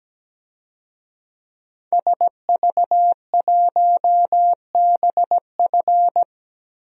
Morse code, heard as a keyed tone, about 17 wpm, 700 Hz. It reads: SV1BF
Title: DXCC-100 Call Signs - 5 Characters - 17wpm